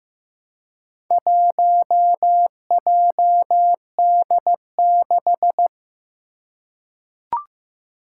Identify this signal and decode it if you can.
Morse code, heard as a keyed tone, about 15 wpm, 700 Hz. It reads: 1JD6
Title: Four Challenging Letter-Number Combinations 15wpm